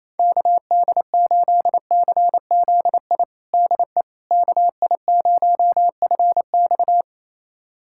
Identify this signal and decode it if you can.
Morse code, heard as a keyed tone, about 28 wpm, 700 Hz. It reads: KD8CZI DE KI0FX
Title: USA Callsign DE Callsign 28wpm